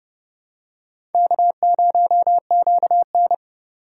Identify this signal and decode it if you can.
Morse code, heard as a keyed tone, about 30 wpm, 700 Hz. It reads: K0QN
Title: USA Call Signs - 4 Characters - 2x Character Spacing - 30wpm